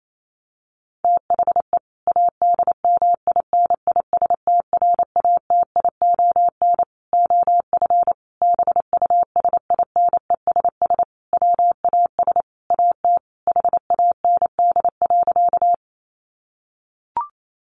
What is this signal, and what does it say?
Morse code, heard as a keyed tone, about 28 wpm, 700 Hz. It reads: THE ADMINISTRATION OF BUSINESS WAS AT HAND.
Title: Sentences from Top 1300 Words 28wpm